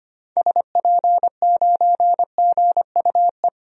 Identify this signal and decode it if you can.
Morse code, heard as a keyed tone, about 25 wpm, 700 Hz. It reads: SP9GUE
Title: DXCC-100 Call Signs - 6 Characters - Repeated 3x - 25wpm